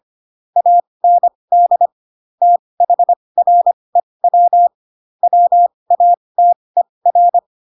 Morse code, heard as a keyed tone, about 25 wpm, spaced as for 20 wpm, 700 Hz. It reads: AND THREW WATER
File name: Pinocchio / Pinocchio ch7 15 25 ik1hge